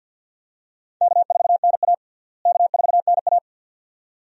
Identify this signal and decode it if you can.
Morse code, heard as a keyed tone, about 52 wpm, 700 Hz. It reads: K4NA K4NA
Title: USA Call Signs - 4 Characters - Repeated 2x - 50wpm